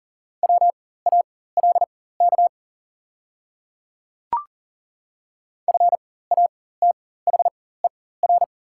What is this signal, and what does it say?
Morse code, heard as a keyed tone, about 40 wpm, spaced as for 19 wpm, 700 Hz. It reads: WALK FATHER
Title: 4x Character Spacing - Top 300 Words - 40wpm